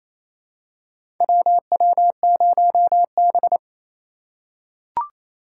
Morse code, heard as a keyed tone, about 28 wpm, 700 Hz. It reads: WW0B
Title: USA Call Signs - 4 Characters - 3x Character Spacing - 28wpm